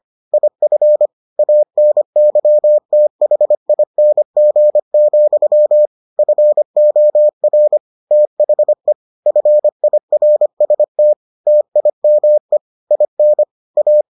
Morse code, heard as a keyed tone, about 25 wpm, 600 Hz. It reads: IF ANYTHING, FOR THE FIRST TIME IN A